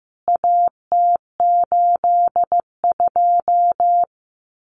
Morse code, heard as a keyed tone, about 15 wpm, 700 Hz. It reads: AT82
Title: CWT Exchange 15wpm